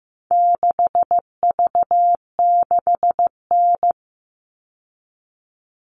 Morse code, heard as a keyed tone, about 15 wpm, 700 Hz. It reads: 6V6N
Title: All Cut Numbers - 4-Digit 15wpm